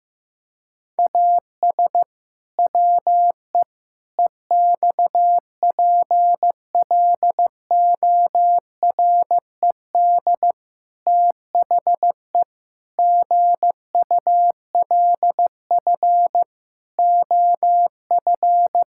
Morse code, heard as a keyed tone, about 15 wpm, 700 Hz. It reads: AS WE EXPLORED THE GULF OF
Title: Pangram Challenge v2 - 15wpm